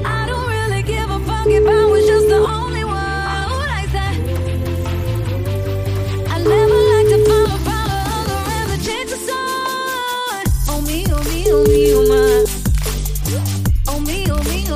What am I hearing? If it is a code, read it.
Ringback tone (Europe, 425 Hz), with noise.